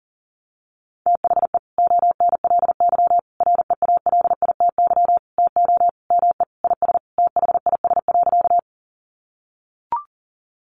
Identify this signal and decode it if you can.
Morse code, heard as a keyed tone, about 40 wpm, 700 Hz. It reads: THE ONLY REALITY TO ME IS THIS.
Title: Sentences from Top 1200 Words 40wpm